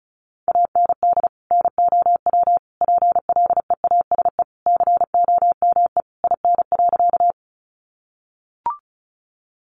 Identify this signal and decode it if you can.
Morse code, heard as a keyed tone, about 34 wpm, 700 Hz. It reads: AND NOW PLEASE COME IN.
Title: Sentences from Top 1500 Words 35wpm